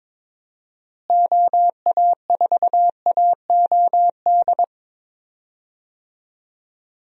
Morse code, heard as a keyed tone, about 22 wpm, 700 Hz. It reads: OA4AOD
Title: USA Call Signs - 6 Characters - Repeated 2x - 22wpm